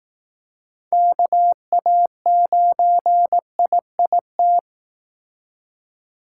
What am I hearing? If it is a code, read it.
Morse code, heard as a keyed tone, about 18 wpm, 700 Hz. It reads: KA9IIT
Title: Speed Racing - USA 2x3 Call Signs 15wpm